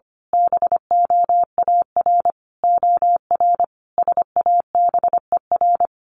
Morse code, heard as a keyed tone, about 25 wpm, 700 Hz. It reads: BOAR OR SABER
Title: A Sound Of Thunder 25wpm